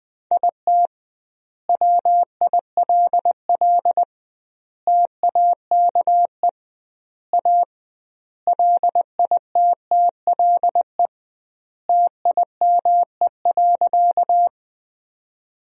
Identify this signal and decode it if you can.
Morse code, heard as a keyed tone, about 20 wpm, 700 Hz. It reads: IT WILL TAKE A LITTLE TIME.
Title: 2x Word-Spacing - Sentences from Top 200 Words - 20wpm